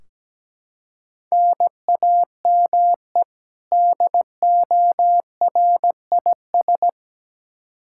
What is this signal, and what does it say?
Morse code, heard as a keyed tone, about 17 wpm, 700 Hz. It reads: NAME DORIS
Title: Practice QSO - Exchange 1 - Beginning - 17wpm